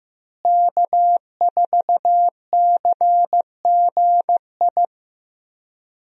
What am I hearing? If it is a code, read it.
Morse code, heard as a keyed tone, about 15 wpm, 700 Hz. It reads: K4CGI
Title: USA Call Signs - 5 Characters - Repeated 3x - 15wpm